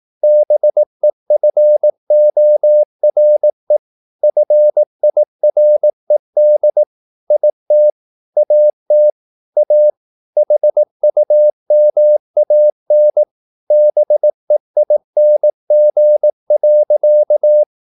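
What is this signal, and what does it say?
Morse code, heard as a keyed tone, about 18 wpm, 600 Hz. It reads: BEFORE FIRED IT AT A HUMAN BEING.
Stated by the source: Morse practice at 18 wpm, English text, from CW med Gnister.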